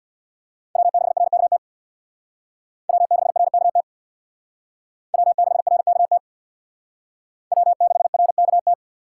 Morse code, heard as a keyed tone, about 62 wpm, 700 Hz. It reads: W6RKT W6RKT W6RKT W6RKT
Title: Speed Racing - USA Call Signs 50wpm